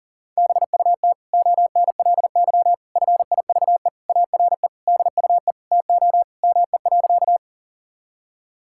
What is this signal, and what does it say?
Morse code, heard as a keyed tone, about 37 wpm, 700 Hz. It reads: BUT ONLY FIVE ARE DUE TO ME.
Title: Sentences from Top 1000 Words - Review Part 6 - 40wpm